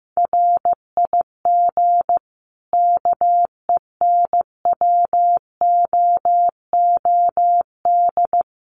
Morse code, heard as a keyed tone, about 15 wpm, 700 Hz. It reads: RIG KENWOOD
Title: Common QSO Phrases 15wpm